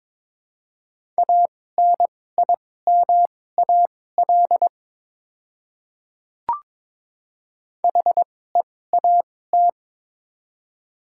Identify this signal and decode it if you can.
Morse code, heard as a keyed tone, about 21 wpm, spaced as for 16 wpm, 700 Hz. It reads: ANIMAL HEAT
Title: 2x Character Spacing - Top 500 Words - 22wpm